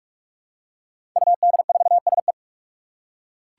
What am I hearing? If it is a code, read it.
Morse code, heard as a keyed tone, about 42 wpm, 700 Hz. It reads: UD4SE
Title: Five Challenging Letter-Number Combinations 45wpm